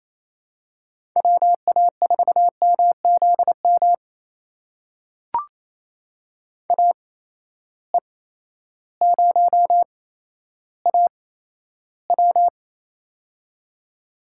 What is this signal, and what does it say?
Morse code, heard as a keyed tone, about 28 wpm, spaced as for 13 wpm, 700 Hz. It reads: WA4MZM AE0AW
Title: USA Call Signs - 8x Character Spacing - 28wpm